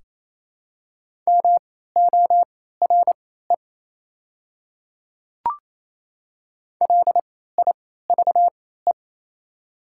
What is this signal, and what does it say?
Morse code, heard as a keyed tone, about 28 wpm, spaced as for 16 wpm, 700 Hz. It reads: MORE LIVE
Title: 3x Character Spacing - Top 200 Words - 28wpm